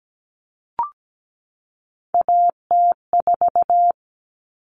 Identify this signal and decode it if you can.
Morse code, heard as a keyed tone, about 17 wpm, 700 Hz. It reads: AT4
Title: Common Cut Numbers - 3-Digit 17wpm